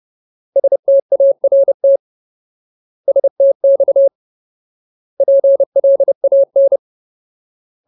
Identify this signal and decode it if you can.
Morse code, heard as a keyed tone, about 30 wpm, 550 Hz. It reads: START STX PLAN
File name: Ultimate Word List / 30WPM Ultimate